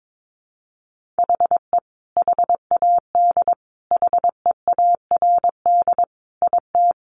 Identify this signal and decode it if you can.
Morse code, heard as a keyed tone, about 22 wpm, 700 Hz. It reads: HE HAD HEARD IT
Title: Sentences from Top 1400 Words 22wpm